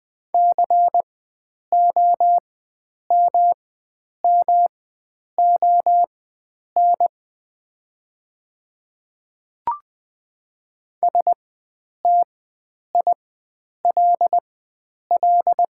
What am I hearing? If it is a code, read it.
Morse code, heard as a keyed tone, about 20 wpm, spaced as for 9 wpm, 700 Hz. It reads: COMMON STILL
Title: 4x Character Spacing - Top 500 Words - 20wpm